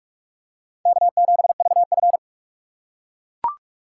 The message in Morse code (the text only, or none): K7VF